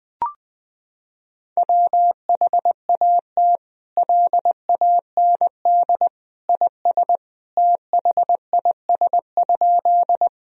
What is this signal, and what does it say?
Morse code, heard as a keyed tone, about 21 wpm, 700 Hz. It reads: WHAT LAND IS THIS?
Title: Sentences from Top 1000 Words - Review Part 1 - 20wpm